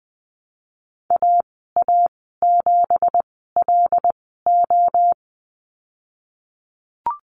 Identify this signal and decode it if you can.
Morse code, heard as a keyed tone, about 20 wpm, spaced as for 15 wpm, 700 Hz. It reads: AA7LO
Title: 173 Lesson 28g - Number 9 - Call Signs 20wpm